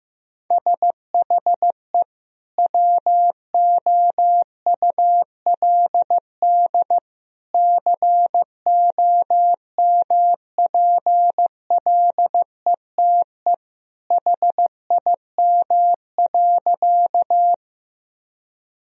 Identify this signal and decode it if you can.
Morse code, heard as a keyed tone, about 15 wpm, 700 Hz. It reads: SHE WOULD COMPLETE HIM.
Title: Sentences from Top 1000 Words - Review Part 2 - 15wpm